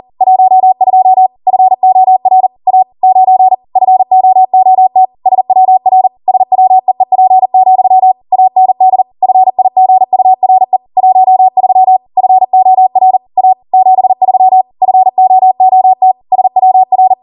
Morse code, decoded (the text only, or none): 12 FOR A 9 FOOT SWR SWEEP, AND FIGURE 13 FOR A 73 FOOT SWR